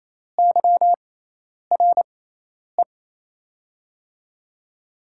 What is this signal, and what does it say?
Morse code, heard as a keyed tone, about 28 wpm, spaced as for 10 wpm, 700 Hz. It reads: YRE
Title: USA Call Signs - 6 Characters - 6x Character Spacing - 28wpm